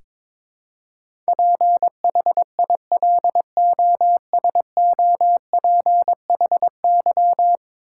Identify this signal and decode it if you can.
Morse code, heard as a keyed tone, about 22 wpm, 700 Hz. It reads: PHILOSOPHY
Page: morse-code-ninja-practice-22wpm